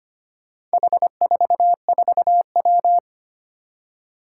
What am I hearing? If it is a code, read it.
Morse code, heard as a keyed tone, about 25 wpm, 700 Hz. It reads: H44W
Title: Four Challenging Letter-Number Combinations 25wpm